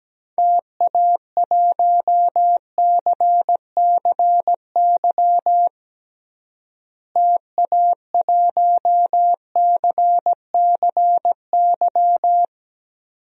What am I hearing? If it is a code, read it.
Morse code, heard as a keyed tone, about 17 wpm, 700 Hz. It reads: TA1CCY TA1CCY
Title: DXCC-100 Call Signs - 6 Characters - Repeated 2x - 17wpm